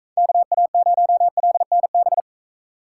Morse code, heard as a keyed tone, about 40 wpm, 700 Hz. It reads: KA0LNB